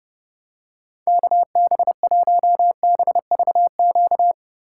Morse code, heard as a keyed tone, about 31 wpm, 700 Hz. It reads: KB1BVQ